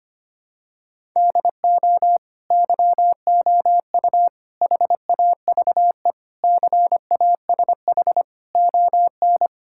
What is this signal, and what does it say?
Morse code, heard as a keyed tone, about 25 wpm, 700 Hz. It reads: DO YOU HAVE CASH ON
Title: Sentences from Top 1000 Words - Review Part 5 - 25wpm